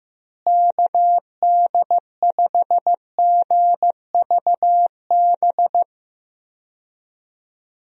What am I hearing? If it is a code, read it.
Morse code, heard as a keyed tone, about 15 wpm, 700 Hz. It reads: KD5GVB
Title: USA Call Signs - 6 Characters - Repeated 2x - 15wpm